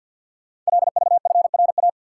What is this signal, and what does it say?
Morse code, heard as a keyed tone, about 48 wpm, 700 Hz. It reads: LVFRA
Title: Code Groups - 5 Letters - 50wpm